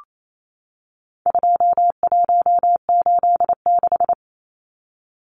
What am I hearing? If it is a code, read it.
Morse code, heard as a keyed tone, about 28 wpm, 700 Hz. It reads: 2186
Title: Numbers - 4-Digit 28wpm